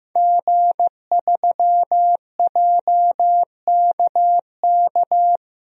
Morse code, heard as a keyed tone, about 15 wpm, 700 Hz. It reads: G3JKK
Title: DXCC-100 Call Signs - 5 Characters - 15wpm